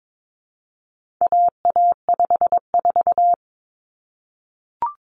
Morse code, heard as a keyed tone, about 22 wpm, 700 Hz. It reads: AA54
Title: Common Cut Numbers - 4-Digit 22wpm